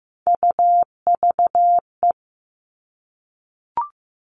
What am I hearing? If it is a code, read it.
Morse code, heard as a keyed tone, about 15 wpm, 700 Hz. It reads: UVE
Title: All Cut Numbers - 3-Digit 15wpm